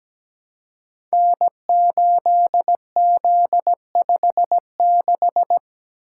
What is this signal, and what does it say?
Morse code, heard as a keyed tone, about 17 wpm, 700 Hz. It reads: N8Z56
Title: Code Groups - 5 Letter:Numbers - 17wpm